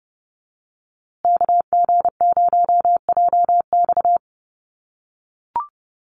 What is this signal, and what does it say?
Morse code, heard as a keyed tone, about 29 wpm, 700 Hz. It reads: KG0JX
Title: USA Call Signs Repeated 3x 30wpm